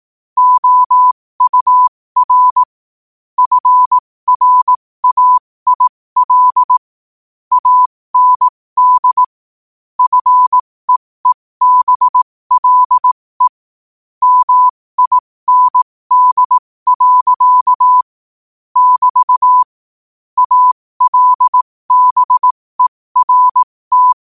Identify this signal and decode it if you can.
Morse code, heard as a keyed tone, about 18 wpm, spaced as for 15 wpm, 1000 Hz. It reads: OUR FRAIL AND FEEBLE MIND. <BT> ALBERT